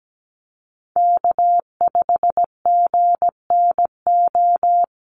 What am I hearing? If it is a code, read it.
Morse code, heard as a keyed tone, about 17 wpm, 700 Hz. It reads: K5GNO
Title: USA Call Signs Repeated 2x 17wpm